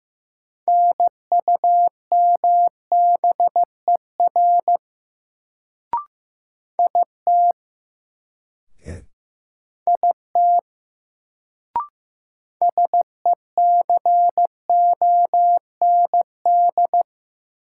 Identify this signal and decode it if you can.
Morse code, heard as a keyed tone, about 16 wpm, 700 Hz. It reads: NUMBER IT IT SECOND
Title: Sets of 2 Words - New Word Drill - 15wpm